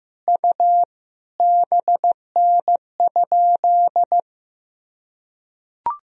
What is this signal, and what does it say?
Morse code, heard as a keyed tone, about 15 wpm, 700 Hz. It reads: U BN?
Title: Sentences from Top 200 Words and Abbreviations 15wpm